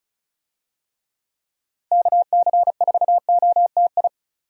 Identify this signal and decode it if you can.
Morse code, heard as a keyed tone, about 36 wpm, 700 Hz. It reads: KC4OTI
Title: USA Call Signs - 6 Characters - Repeated 2x - 35wpm